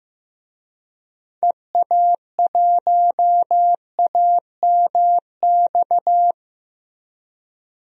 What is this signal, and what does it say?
Morse code, heard as a keyed tone, about 15 wpm, 700 Hz. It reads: EA1AMX
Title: DXCC-100 Call Signs - 6 Characters - 15wpm